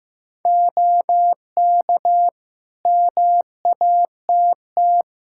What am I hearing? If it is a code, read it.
Morse code, heard as a keyed tone, about 15 wpm, 700 Hz. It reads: OK MATT
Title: Practice QSO - Exchange 3 - More Info - 15wpm